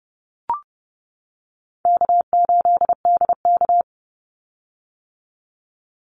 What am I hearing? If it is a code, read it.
Morse code, heard as a keyed tone, about 30 wpm, 700 Hz. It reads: K8DK
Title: Easy Call Signs - 30wpm